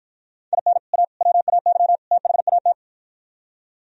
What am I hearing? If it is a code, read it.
Morse code, heard as a keyed tone, about 52 wpm, 700 Hz. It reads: IN A WAY THAT